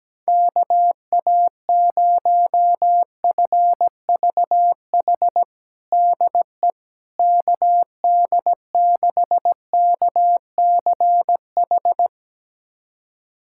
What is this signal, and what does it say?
Morse code, heard as a keyed tone, about 17 wpm, 700 Hz. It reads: KA0FVH DE KD6KCH
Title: USA Callsign DE Callsign - 6 Characters - 17wpm